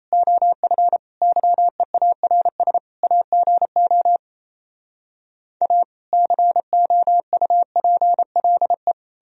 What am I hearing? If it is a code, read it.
Morse code, heard as a keyed tone, about 38 wpm, 700 Hz. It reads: OF YEARS AGO A COUPLE